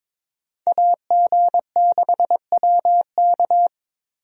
Morse code, heard as a keyed tone, about 22 wpm, 700 Hz. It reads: AG6WK
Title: USA Call Signs - 5 Characters - Repeated 3x - 22wpm